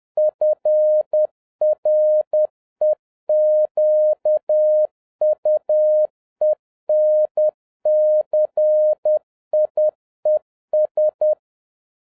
Morse code, heard as a keyed tone, about 10 wpm, 610 Hz. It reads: FREQUENCIES
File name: 2011-07-mp3 / n56 BACK FROM A BIG ACCIDENT 10wpm